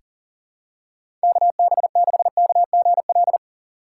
Morse code, heard as a keyed tone, about 39 wpm, 700 Hz. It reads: KB6KGL
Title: USA Call Signs - 3x Character Spacing - 40wpm